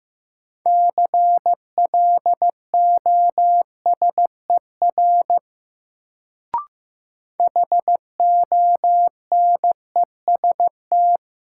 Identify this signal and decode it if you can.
Morse code, heard as a keyed tone, about 16 wpm, 700 Hz. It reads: CLOSER HONEST
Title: Sets of 5 Words Encore - New Word Drill - 15wpm